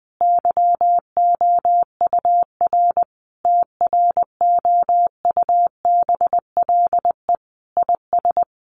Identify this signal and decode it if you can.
Morse code, heard as a keyed tone, about 20 wpm, 700 Hz. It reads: YOUR TROUBLE IS
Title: Sentences from Top 1200 Words 20wpm